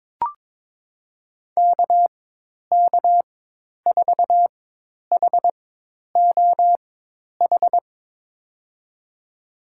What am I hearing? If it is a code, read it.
Morse code, heard as a keyed tone, about 22 wpm, spaced as for 10 wpm, 700 Hz. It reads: KK4HOH
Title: USA Call Signs - 6 Characters - 4x Character Spacing - 22wpm